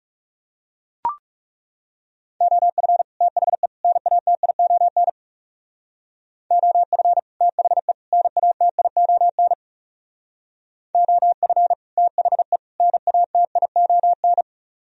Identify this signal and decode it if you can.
Morse code, heard as a keyed tone, about 38 wpm, 700 Hz. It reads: OF THE NATION OF THE NATION OF THE NATION
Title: Speed-Racing - Sets of 3 Words 30wpm